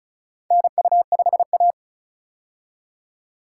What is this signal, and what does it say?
Morse code, heard as a keyed tone, about 36 wpm, 700 Hz. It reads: NU5A
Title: K1USN SST QSO Practice - 35wpm